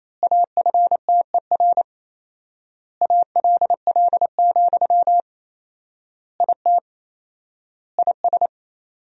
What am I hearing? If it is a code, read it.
Morse code, heard as a keyed tone, about 28 wpm, 700 Hz. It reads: AFTER ALL, IT IS
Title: 4x Word-Spacing - Sentences from Top 500 Words - 28wpm